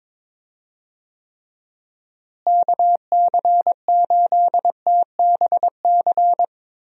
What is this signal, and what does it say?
Morse code, heard as a keyed tone, about 22 wpm, 700 Hz. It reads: KC8TBC
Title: USA Call Signs - 6 Characters - Repeated 2x - 22wpm